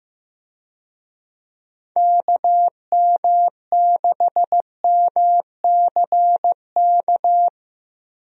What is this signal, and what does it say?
Morse code, heard as a keyed tone, about 15 wpm, 700 Hz. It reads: KM6MCK
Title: USA Call Signs - 6 Characters - Repeated 2x - 15wpm